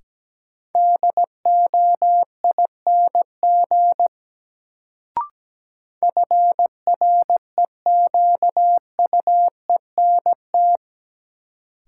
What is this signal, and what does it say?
Morse code, heard as a keyed tone, about 17 wpm, 700 Hz. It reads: DOING FREQUENT
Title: Sentences from Top 1400 Words - New Word Drill - 17wpm